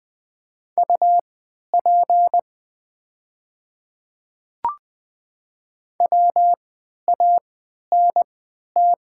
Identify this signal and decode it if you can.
Morse code, heard as a keyed tone, about 20 wpm, spaced as for 11 wpm, 690 Hz. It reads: UP WANT
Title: 3x Character Spacing - Top 200 Words - 20wpm